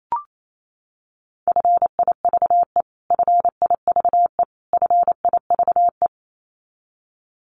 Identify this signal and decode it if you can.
Morse code, heard as a keyed tone, about 28 wpm, 700 Hz. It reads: FIVE FIVE FIVE
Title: Top 500 Words Repeated 28wpm